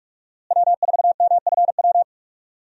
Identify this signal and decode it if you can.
Morse code, heard as a keyed tone, about 46 wpm, 700 Hz. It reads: W4MFW
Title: USA Call Signs - 5 Characters - Repeated 2x - 45wpm